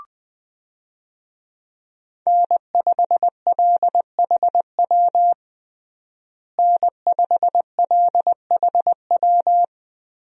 Morse code, heard as a keyed tone, about 20 wpm, 700 Hz. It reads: N5LHW N5LHW
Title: USA Call Signs - 5 Characters - Repeated 3x - 20wpm